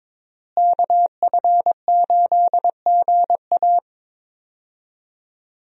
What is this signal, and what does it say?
Morse code, heard as a keyed tone, about 22 wpm, 700 Hz. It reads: KF8GA
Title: USA Call Signs - 5 Characters - Repeated 2x - 22wpm